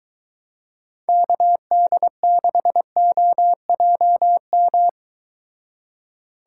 Morse code, heard as a keyed tone, about 22 wpm, 700 Hz. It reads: KD6OJM